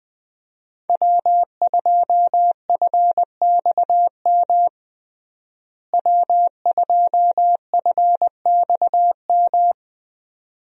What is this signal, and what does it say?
Morse code, heard as a keyed tone, about 21 wpm, 700 Hz. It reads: W2FXM W2FXM
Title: USA Call Signs - 5 Characters - Repeated 2x - 20wpm